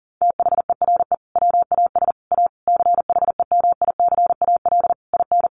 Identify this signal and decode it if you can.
Morse code, heard as a keyed tone, about 40 wpm, 700 Hz. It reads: THERE WAS A CHEMICAL IN
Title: Sentences from Top 1200 Words 40wpm